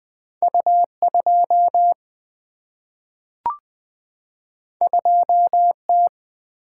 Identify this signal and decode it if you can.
Morse code, heard as a keyed tone, about 20 wpm, 700 Hz. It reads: U2 2T
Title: Single-Letter Single-Number 20wpm